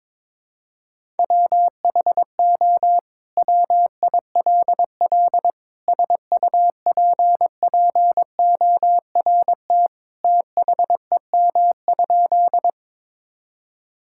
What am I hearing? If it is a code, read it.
Morse code, heard as a keyed tone, about 22 wpm, 700 Hz. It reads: WHO WILL SUPPORT THEM?
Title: Sentences from Top 1000 Words - Review Part 5 - 22wpm